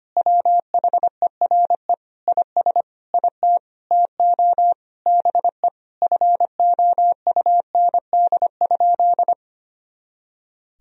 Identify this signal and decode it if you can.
Morse code, heard as a keyed tone, about 25 wpm, 700 Hz. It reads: WHERE IS IT TO BE FOUND?
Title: Sentences from Top 300 Words and Abbreviations 25wpm